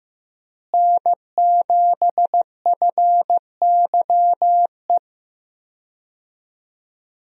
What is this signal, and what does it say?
Morse code, heard as a keyed tone, about 15 wpm, 700 Hz. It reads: N7FYE
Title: USA Call Signs - 5 Characters - Repeated 3x - 15wpm